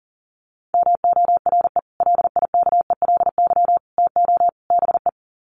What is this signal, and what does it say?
Morse code, heard as a keyed tone, about 40 wpm, 700 Hz. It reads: MORE LIKELY TO BE
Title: Sets of 4 Words 40wpm